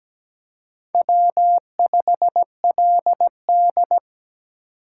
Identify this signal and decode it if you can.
Morse code, heard as a keyed tone, about 17 wpm, 700 Hz. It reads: W5LD
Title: USA Call Signs - 4 Characters - 3x Character Spacing - 17wpm